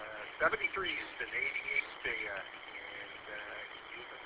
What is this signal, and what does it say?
Emergency Alert System (SAME) tones, heard as harsh buzzing bursts, with speech. It is heard from 1.0 to 3.6 s.